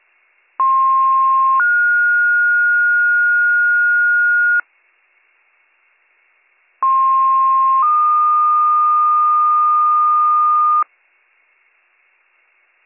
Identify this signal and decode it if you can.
Two-tone page, heard as a one-second tone, then a three-second tone, about 1060 then 1430 Hz, about 1060 then 1220 Hz. It is heard from 0.6 to 10.8 s.